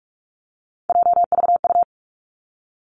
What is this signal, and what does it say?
Morse code, heard as a keyed tone, about 46 wpm, 700 Hz. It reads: JVU
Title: Three Challenging Letter-Number Combinations 45wpm